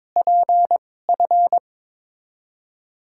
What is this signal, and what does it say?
Morse code, heard as a keyed tone, about 23 wpm, spaced as for 16 wpm, 700 Hz. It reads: PF
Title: USA Call Signs - 6 Characters - 2x Character Spacing - 22wpm